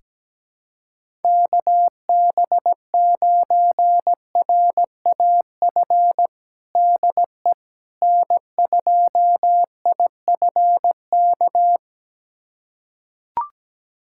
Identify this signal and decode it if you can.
Morse code, heard as a keyed tone, about 17 wpm, 700 Hz. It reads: KB9RAF DE N2IFK
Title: USA Callsign DE Callsign 17wpm